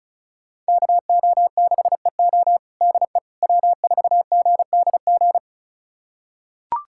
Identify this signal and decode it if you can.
Morse code, heard as a keyed tone, about 34 wpm, 700 Hz. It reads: KO6EO DE W4GDG
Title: USA Callsign DE Callsign - 5 Characters - 35wpm